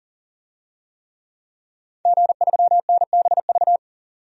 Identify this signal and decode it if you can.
Morse code, heard as a keyed tone, about 38 wpm, 700 Hz. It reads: G3NBV
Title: DXCC-100 Call Signs - 5 Characters - 40wpm